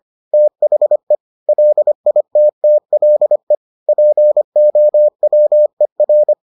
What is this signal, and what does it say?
Morse code, heard as a keyed tone, about 25 wpm, 600 Hz. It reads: THE LITTLE POWER